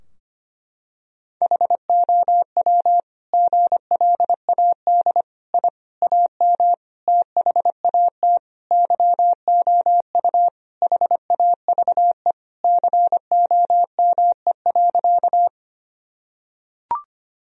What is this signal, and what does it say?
Morse code, heard as a keyed tone, about 25 wpm, 700 Hz. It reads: HOW GLAD I AM THAT YOU HAVE COME.
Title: Sentences from Top 1000 Words - Review Part 2 - 25wpm